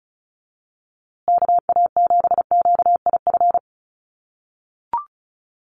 Morse code, heard as a keyed tone, about 34 wpm, 700 Hz. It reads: KA7QIF
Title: USA Call Signs Repeated 2x 35wpm